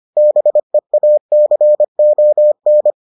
Morse code, heard as a keyed tone, about 25 wpm, 600 Hz. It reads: BEACON